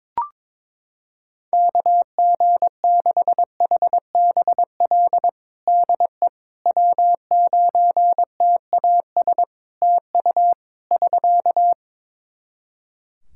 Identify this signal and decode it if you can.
Morse code, heard as a keyed tone, about 22 wpm, 700 Hz. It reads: KG6HBL DE W9TAS TU <SK>
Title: Practice QSO - Exchange 4 - Ending - 22wpm